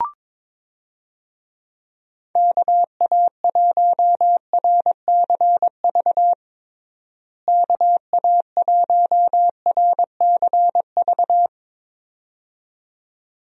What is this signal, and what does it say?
Morse code, heard as a keyed tone, about 22 wpm, 700 Hz. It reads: KA1RCV KA1RCV
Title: USA Call Signs - 6 Characters - Repeated 2x - 22wpm